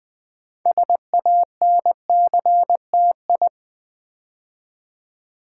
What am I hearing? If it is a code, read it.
Morse code, heard as a keyed tone, about 20 wpm, 700 Hz. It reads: SANCTI
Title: Build A Word - 2 Letters At A Time - 8-Letter Words - 20wpm